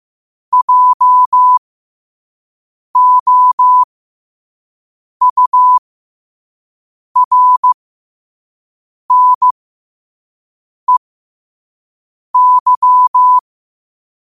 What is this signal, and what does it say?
Morse code, heard as a keyed tone, about 15 wpm, spaced as for 5 wpm, 1000 Hz. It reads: JOURNEY
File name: qotdfiles / 5wpm